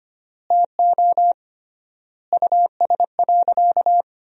Morse code, heard as a keyed tone, about 25 wpm, 700 Hz. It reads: TO US.
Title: 3x Word-Spacing - Sentences from Top 500 Words - 25wpm